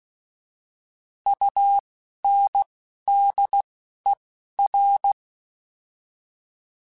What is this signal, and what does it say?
Morse code, heard as a keyed tone, about 16 wpm, spaced as for 11 wpm, 800 Hz. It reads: UNDER
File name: Alphabetical / Alphabetical mp3s / Us @10